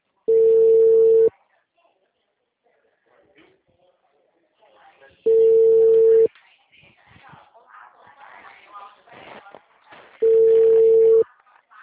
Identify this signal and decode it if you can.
Ringback tone (China, 450 Hz), with speech.